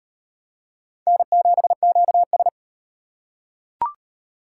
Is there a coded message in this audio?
Yes: Morse code from 1.1 to 2.5 s, heard as a keyed tone, about 37 wpm, 700 Hz. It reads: N7QS